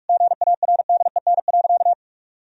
Morse code, heard as a keyed tone, about 46 wpm, 700 Hz. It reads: GARDEN.
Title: Sentences from Top 1000 Words - Review Part 4 - 45wpm